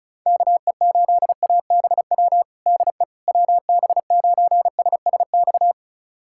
Morse code, heard as a keyed tone, about 34 wpm, 700 Hz. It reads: KE8ABW DE WB9SSX
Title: USA Callsign DE Callsign - 6 Characters - 35wpm